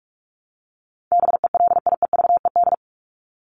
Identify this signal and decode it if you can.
Morse code, heard as a keyed tone, about 44 wpm, 700 Hz. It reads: BELIEVED
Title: ED Suffix - Top 100 Words with Base Words 45wpm